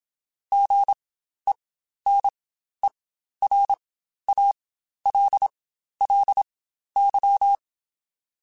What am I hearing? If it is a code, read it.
Morse code, heard as a keyed tone, about 27 wpm, spaced as for 12 wpm, 800 Hz. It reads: GENERALLY
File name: Spaced Characters / Spaced Characters-mp3s / Weather4 14 sp @25